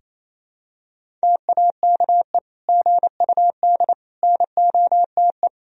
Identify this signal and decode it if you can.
Morse code, heard as a keyed tone, about 28 wpm, 700 Hz. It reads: TAKE GUD NOTE
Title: Sentences from Top 500 Words and Abbreviations 28wpm